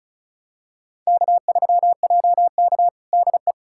Morse code, heard as a keyed tone, about 34 wpm, 700 Hz. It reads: K3JK DE
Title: USA Callsign DE Callsign - 4 Characters - 35wpm